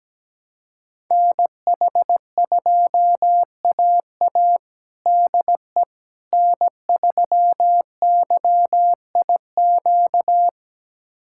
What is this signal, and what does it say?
Morse code, heard as a keyed tone, about 17 wpm, 700 Hz. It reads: NH2AA DE N3YIQ
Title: USA Callsign DE Callsign - 5 Characters - 17wpm